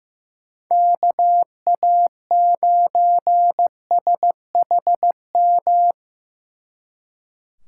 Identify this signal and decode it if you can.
Morse code, heard as a keyed tone, about 18 wpm, 700 Hz. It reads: KA9SHM
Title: Speed Racing - USA 2x3 Call Signs 15wpm